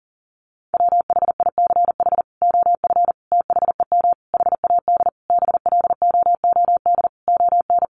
Morse code, heard as a keyed tone, about 40 wpm, 700 Hz. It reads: WHICH OF THEM HAD BLOOD ON
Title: Sentences from Top 1100 Words 40wpm